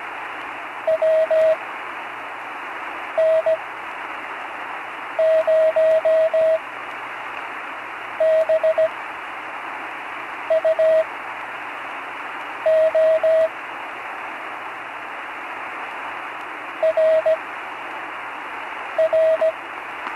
Morse code, heard as a keyed tone, about 17 wpm, spaced as for 5 wpm, 640 Hz, with radio noise. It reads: WN0BUO RR
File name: CW Learning Programs / Learning Morse Code (from Tune in the World) / 43 on the air